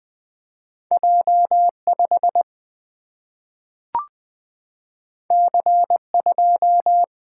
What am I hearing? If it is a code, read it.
Morse code, heard as a keyed tone, about 20 wpm, 700 Hz. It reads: J 5 C 2
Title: Single-Letter Single-Number 20wpm